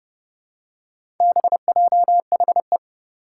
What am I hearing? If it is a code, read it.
Morse code, heard as a keyed tone, about 32 wpm, 700 Hz. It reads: BJHE
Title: Four Challenging Letter-Number Combinations 30wpm